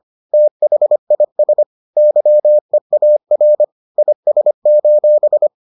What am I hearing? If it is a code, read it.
Morse code, heard as a keyed tone, about 25 wpm, 600 Hz. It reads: THIS YEAR IS: